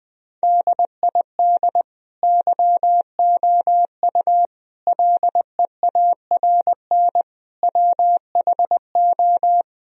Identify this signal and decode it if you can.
Morse code, heard as a keyed tone, about 21 wpm, 700 Hz. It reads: DID YOU LEARN WHO
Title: Sentences from Top 1000 Words - Review Part 1 - 20wpm